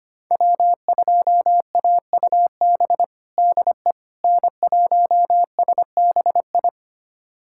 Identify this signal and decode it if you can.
Morse code, heard as a keyed tone, about 25 wpm, 700 Hz. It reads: W2AUB DE N1SBI
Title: USA Callsign DE Callsign 25wpm